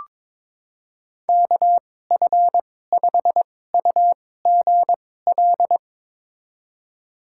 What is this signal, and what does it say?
Morse code, heard as a keyed tone, about 22 wpm, spaced as for 16 wpm, 700 Hz. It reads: KF5UGL
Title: USA Call Signs - 2x Character Spacing - 22wpm